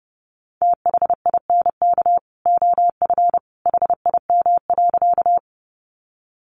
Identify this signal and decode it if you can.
Morse code, heard as a keyed tone, about 30 wpm, 700 Hz. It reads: THINK OF HIM.
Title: Sentences from Top 200 Words 30wpm